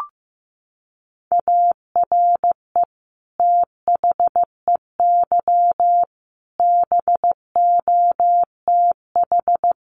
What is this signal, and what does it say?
Morse code, heard as a keyed tone, about 15 wpm, 700 Hz. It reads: ARE THEY BOTH
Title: Sentences from Top 1500 Words 15wpm